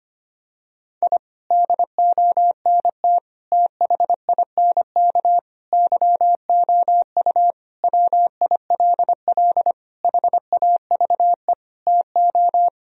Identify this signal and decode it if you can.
Morse code, heard as a keyed tone, about 25 wpm, 700 Hz. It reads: I DONT THINK YOU WILL HAVE TO
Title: Sentences from Top 1000 Words - Review Part 4 - 25wpm